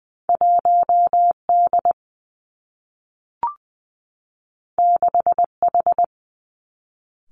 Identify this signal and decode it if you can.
Morse code, heard as a keyed tone, about 19 wpm, 700 Hz. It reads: 1 D 6 H
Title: Two Challenging Letter-Number Combinations 20wpm